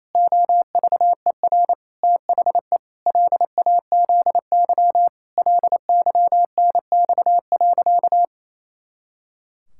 Morse code, heard as a keyed tone, about 28 wpm, 700 Hz. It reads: OVER THE LAZY LYNX.
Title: Pangram Challenge v2 - 28wpm